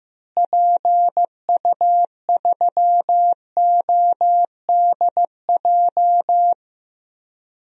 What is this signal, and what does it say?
Morse code, heard as a keyed tone, about 15 wpm, 700 Hz. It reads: PU3ODJ
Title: DXCC-100 Call Signs - 6 Characters - Repeated 3x - 15wpm